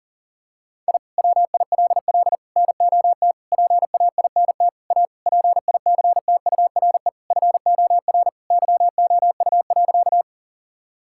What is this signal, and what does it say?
Morse code, heard as a keyed tone, about 40 wpm, 700 Hz. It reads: I WILL NOT PAINT A PICTURE FOR YOU.
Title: Sentences from Top 1000 Words - Review Part 6 - 40wpm